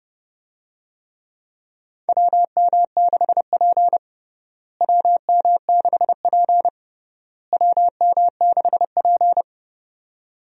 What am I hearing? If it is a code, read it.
Morse code, heard as a keyed tone, about 30 wpm, 700 Hz. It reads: WM6P WM6P WM6P
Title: USA Call Signs - 4 Characters - Repeated 3x - 30wpm